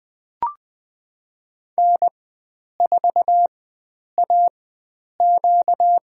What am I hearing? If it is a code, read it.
Morse code, heard as a keyed tone, about 20 wpm, spaced as for 9 wpm, 700 Hz. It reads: N4AQ
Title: USA Call Signs - 4 Characters - 4x Character Spacing - 20wpm